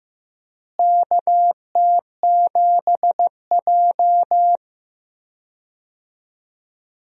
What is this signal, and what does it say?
Morse code, heard as a keyed tone, about 16 wpm, 700 Hz. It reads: KT7J
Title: USA Call Signs - 4 Characters - Repeated 2x - 15wpm